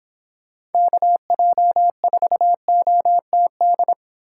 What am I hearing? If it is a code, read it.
Morse code, heard as a keyed tone, about 24 wpm, 700 Hz. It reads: KJ4OTD